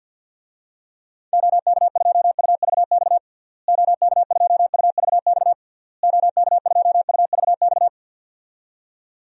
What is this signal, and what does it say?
Morse code, heard as a keyed tone, about 50 wpm, 700 Hz. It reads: OK2UVX OK2UVX OK2UVX
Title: DXCC-100 Call Signs - 6 Characters - Repeated 3x - 50wpm